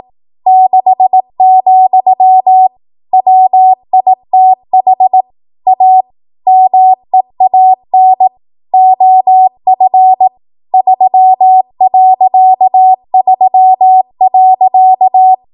Morse code, heard as a keyed tone, about 18 wpm, 750 Hz. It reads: 6, WITH A MEAN OF 3.3.
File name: Morse / 18WPM